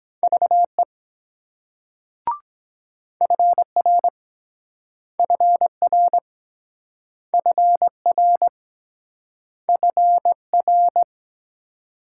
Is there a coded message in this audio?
Yes: Morse code from 0.2 to 11.0 s, heard as a keyed tone, about 23 wpm, 700 Hz. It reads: VE FR FR FR FR